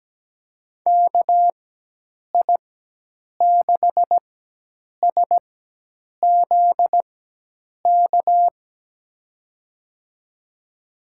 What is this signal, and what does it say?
Morse code, heard as a keyed tone, about 17 wpm, spaced as for 8 wpm, 700 Hz. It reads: KI6SZK
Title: USA Call Signs - 6 Characters - 4x Character Spacing - 17wpm